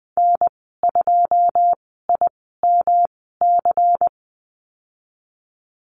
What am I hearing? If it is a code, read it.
Morse code, heard as a keyed tone, about 20 wpm, spaced as for 15 wpm, 700 Hz. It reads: N2IMC
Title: 167 Lesson 27g - Number 7 - Call Signs 20wpm